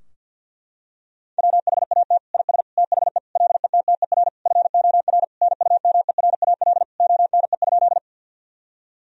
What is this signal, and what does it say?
Morse code, heard as a keyed tone, about 52 wpm, 700 Hz. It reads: WHAT IS THE LETTER FOR NUMERAL ONE?
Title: Sentences from Top 1000 Words - Review Part 2 - 50wpm